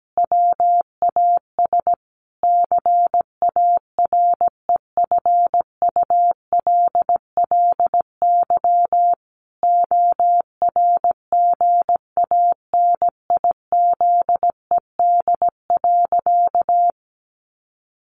Morse code, heard as a keyed tone, about 17 wpm, 700 Hz. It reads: WAS CAREFULLY ORGANIZED.
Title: Sentences from Top 1300 Words 17wpm